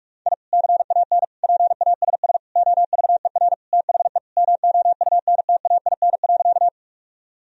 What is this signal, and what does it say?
Morse code, heard as a keyed tone, about 46 wpm, 700 Hz. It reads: I CAN PASS OVER THE MOUNTAIN.
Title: Sentences from Top 1000 Words - Review Part 1 - 45wpm